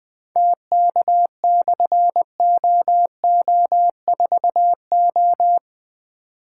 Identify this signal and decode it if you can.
Morse code, heard as a keyed tone, about 20 wpm, 700 Hz. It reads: TK/OO4O
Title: Difficult Call Signs 20wpm